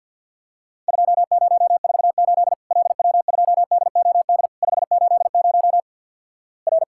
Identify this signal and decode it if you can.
Morse code, heard as a keyed tone, about 50 wpm, 700 Hz. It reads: 2048 LW2DOD 580 R
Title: Worked All Europe DX Contest - QTC Exchange - 50wpm